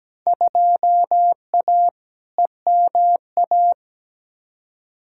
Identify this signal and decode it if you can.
Morse code, heard as a keyed tone, about 17 wpm, 700 Hz. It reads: 2A EMA
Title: Field Day Exchanges - 17wpm